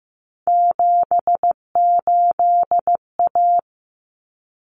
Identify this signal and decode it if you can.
Morse code, heard as a keyed tone, about 15 wpm, 700 Hz. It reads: 78A
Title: Common Cut Numbers - 3-Digit 15wpm